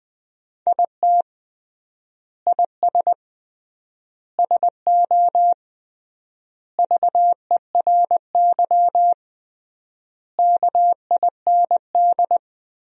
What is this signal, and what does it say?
Morse code, heard as a keyed tone, about 20 wpm, 700 Hz. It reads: IT IS SO VERY KIND
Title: 3x Word-Spacing - Sentences from Top 300 Words - 20wpm